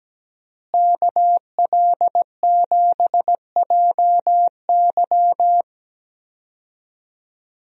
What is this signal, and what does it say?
Morse code, heard as a keyed tone, about 17 wpm, 700 Hz. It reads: KL7JY
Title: DXCC-100 Call Signs - 5 Characters - Repeated 2x - 17wpm